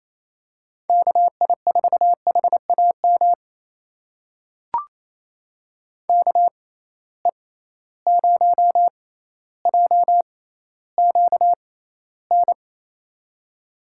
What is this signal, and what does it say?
Morse code, heard as a keyed tone, about 28 wpm, spaced as for 14 wpm, 700 Hz. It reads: KI4HAM KE0JQN
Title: USA Call Signs - 6x Character Spacing - 28wpm